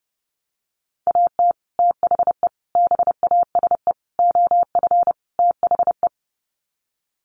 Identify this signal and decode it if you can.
Morse code, heard as a keyed tone, about 30 wpm, 700 Hz. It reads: AT THE BASE OF THE
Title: Sets of 5 Words 30wpm